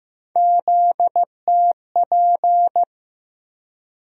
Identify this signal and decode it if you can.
Morse code, heard as a keyed tone, about 15 wpm, 700 Hz. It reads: ZTP
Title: USA Call Sign Postfixes - 3 Letters 15wpm